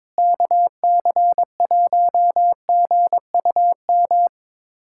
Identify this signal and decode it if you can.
Morse code, heard as a keyed tone, about 22 wpm, 700 Hz. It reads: KC1GUM
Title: USA Callsign DE Callsign - 6 Characters - 22wpm